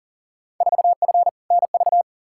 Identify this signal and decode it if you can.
Morse code, heard as a keyed tone, about 37 wpm, 700 Hz. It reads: 4F NV
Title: Field Day Exchanges Encore - 40wpm